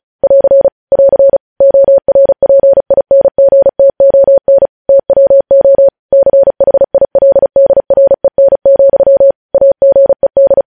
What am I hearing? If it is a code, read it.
Morse code, heard as a keyed tone, about 36 wpm, 550 Hz. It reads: <AR> <AR> ORPINGTON TWO CHILDREN, AGED